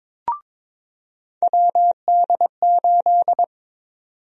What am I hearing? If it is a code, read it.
Morse code, heard as a keyed tone, about 22 wpm, 700 Hz. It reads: WD8
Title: USA Call Sign Prefixes - 22wpm